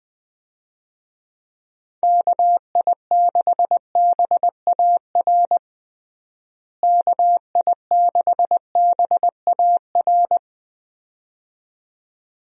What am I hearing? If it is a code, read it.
Morse code, heard as a keyed tone, about 20 wpm, 700 Hz. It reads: KI6BAR KI6BAR
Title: USA Call Signs - 6 Characters - Repeated 2x - 20wpm